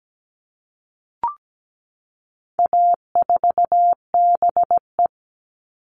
Morse code, heard as a keyed tone, about 17 wpm, 700 Hz. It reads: A4BE
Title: All Cut Numbers - 4-Digit 17wpm